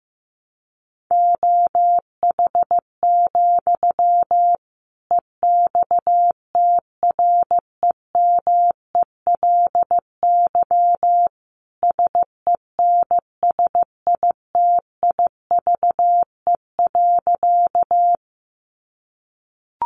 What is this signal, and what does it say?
Morse code, heard as a keyed tone, about 15 wpm, 700 Hz. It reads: OH, EXTREMELY SENSITIVE.
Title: Sentences from Top 1400 Words 15wpm